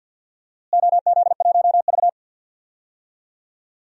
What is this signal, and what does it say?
Morse code, heard as a keyed tone, about 48 wpm, 700 Hz. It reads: OZ1V
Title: DXCC-100 Call Signs - 4 Characters - Repeated 2x - 50wpm